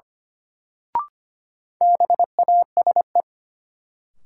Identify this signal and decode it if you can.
Morse code, heard as a keyed tone, about 25 wpm, 700 Hz. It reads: BASE
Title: Sentences from Top 1000 Words - Review Part 1 - New Word Drill - 25wpm